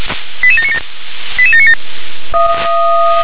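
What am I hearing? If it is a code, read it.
Five-tone selective call (ZVEI), heard as quick runs of five tones, tones of about 70 ms. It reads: 88797 88797
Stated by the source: ZVEI five-tone call received by SDR.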